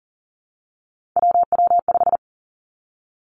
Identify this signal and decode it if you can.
Morse code, heard as a keyed tone, about 40 wpm, 700 Hz. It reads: WW5
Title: Three Challenging Letter-Number Combinations 40wpm